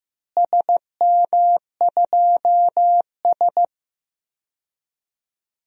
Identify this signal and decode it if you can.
Morse code, heard as a keyed tone, about 15 wpm, 700 Hz. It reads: SM2S
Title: DXCC-100 Call Signs - 4 Characters - Repeated 3x - 15wpm